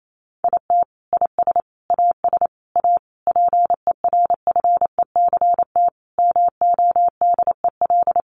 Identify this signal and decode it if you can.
Morse code, heard as a keyed tone, about 28 wpm, 700 Hz. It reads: IT IS AS A PERFECT MODEL
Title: Sentences from Top 1000 Words 28wpm